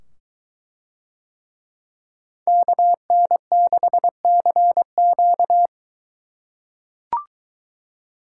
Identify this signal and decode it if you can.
Morse code, heard as a keyed tone, about 23 wpm, 700 Hz. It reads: KN6CQ